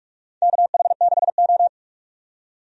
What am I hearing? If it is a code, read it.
Morse code, heard as a keyed tone, about 44 wpm, 700 Hz. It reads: KH6O
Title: DXCC-100 Call Signs - 4 Characters - 45wpm